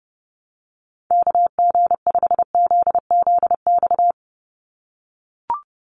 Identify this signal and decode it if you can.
Morse code, heard as a keyed tone, about 30 wpm, 700 Hz. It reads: KG5ZZX
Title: USA Call Signs 30wpm